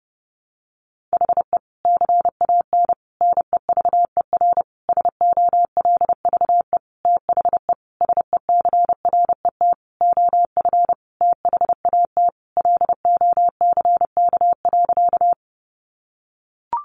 Morse code, heard as a keyed tone, about 30 wpm, 700 Hz. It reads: HE CAN NEVER SOLVE THE SECRET OF THAT LOCK.
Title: Sentences from Top 1300 Words 30wpm